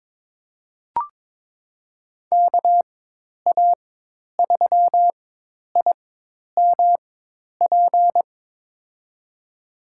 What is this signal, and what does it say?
Morse code, heard as a keyed tone, about 22 wpm, spaced as for 10 wpm, 700 Hz. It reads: KA3IMP
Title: USA Call Signs - 6 Characters - 4x Character Spacing - 22wpm